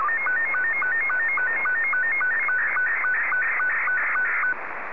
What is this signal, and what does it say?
SSTV (slow-scan television), Scottie 2.